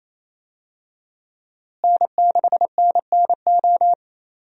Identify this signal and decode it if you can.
Morse code, heard as a keyed tone, about 27 wpm, 700 Hz. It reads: N6NNO
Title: USA Call Signs - 5 Characters - Repeated 3x - 28wpm